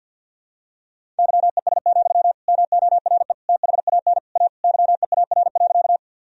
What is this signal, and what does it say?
Morse code, heard as a keyed tone, about 51 wpm, 700 Hz. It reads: YES, MORE THAN A YEAR.
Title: Sentences from Top 1000 Words - Review Part 1 - 50wpm